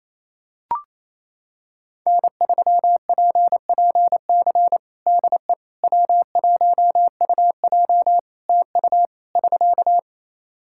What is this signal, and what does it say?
Morse code, heard as a keyed tone, about 28 wpm, 700 Hz. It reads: N3PPC DE W1UJ TU <SK>
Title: Practice QSO - Exchange 4 - Ending - 28wpm